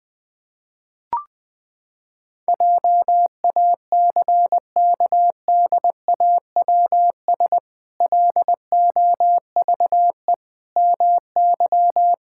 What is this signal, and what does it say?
Morse code, heard as a keyed tone, about 20 wpm, 700 Hz. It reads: JACKDAWS LOVE MY
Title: Pangram Challenge v2 - 20wpm